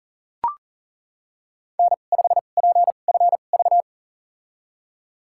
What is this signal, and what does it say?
Morse code, heard as a keyed tone, about 37 wpm, spaced as for 28 wpm, 700 Hz. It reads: N5PFV
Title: USA Call Signs - 2x Character Spacing - 40wpm